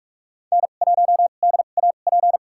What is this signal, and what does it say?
Morse code, heard as a keyed tone, about 42 wpm, spaced as for 31 wpm, 690 Hz. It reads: N1DAP
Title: USA Call Signs - 5 Characters - 2x Character Spacing - 45wpm